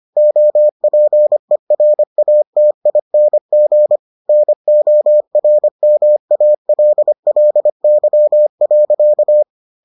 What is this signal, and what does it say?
Morse code, heard as a keyed tone, about 25 wpm, 600 Hz. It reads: OPERATING NORMALLY.